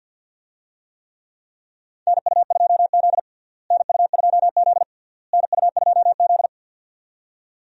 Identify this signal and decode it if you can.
Morse code, heard as a keyed tone, about 52 wpm, 700 Hz. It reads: NU2Z NU2Z NU2Z
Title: USA Call Signs - 4 Characters - Repeated 3x - 50wpm